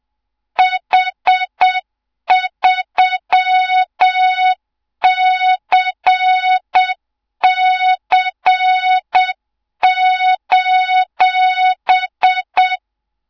Morse code, heard as a keyed tone, about 7 wpm, 3700 Hz, with noise. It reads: H3CC:
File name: Morze-HA2MN / hang / 24 GYAKORLAT